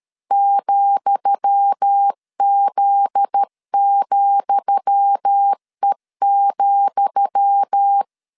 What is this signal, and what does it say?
Morse code, heard as a keyed tone, about 13 wpm, 800 Hz. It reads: ,Z,E,
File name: LEZIONI / 10 SIGNS / 10D COMA 12wpm 2544